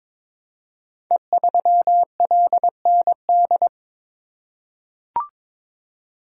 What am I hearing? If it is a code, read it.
Morse code, heard as a keyed tone, about 23 wpm, 690 Hz. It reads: E3LND
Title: Code Groups - 5 Letter:Numbers - 22wpm